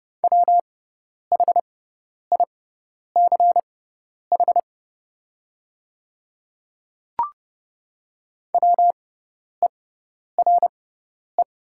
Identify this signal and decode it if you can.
Morse code, heard as a keyed tone, about 30 wpm, spaced as for 10 wpm, 700 Hz. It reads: WHICH WERE
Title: 6x Character Spacing - Top 100 Words - 30wpm